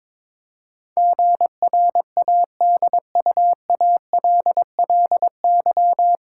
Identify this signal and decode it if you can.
Morse code, heard as a keyed tone, about 22 wpm, 700 Hz. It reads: GRADUALLY
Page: Morse Code Ninja practice 22wpm